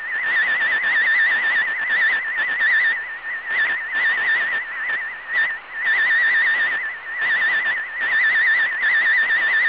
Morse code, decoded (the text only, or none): none